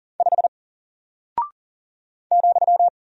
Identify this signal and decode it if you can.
Morse code, heard as a keyed tone, about 40 wpm, 700 Hz. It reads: 5 ,
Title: Single Character (Letter:Number:Punctuation:Prosign) Alt 40wpm